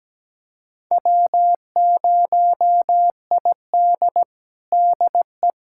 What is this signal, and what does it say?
Morse code, heard as a keyed tone, about 17 wpm, 700 Hz. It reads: W0ID DE
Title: USA Callsign DE Callsign - 4 Characters - 17wpm